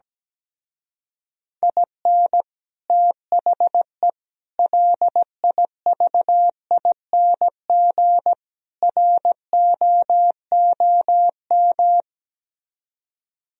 Morse code, heard as a keyed tone, about 17 wpm, 700 Hz. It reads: IN THE LIVING ROOM
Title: Speed-Racing - Sets of 4 Words 17wpm